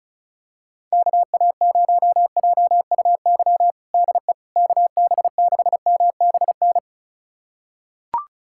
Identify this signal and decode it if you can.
Morse code, heard as a keyed tone, about 36 wpm, 700 Hz. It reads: KA0JUY DE KB6MBN
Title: USA Callsign DE Callsign 35wpm